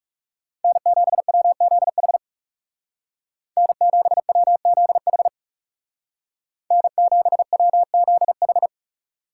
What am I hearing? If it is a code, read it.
Morse code, heard as a keyed tone, about 39 wpm, 700 Hz. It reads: N7WZH N7WZH N7WZH